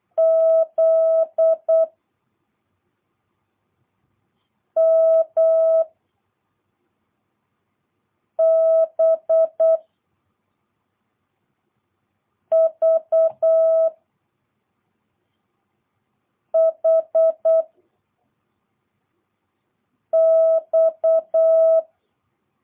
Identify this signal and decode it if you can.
Morse code, heard as a keyed tone, about 8 wpm, 650 Hz. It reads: Z M B V H X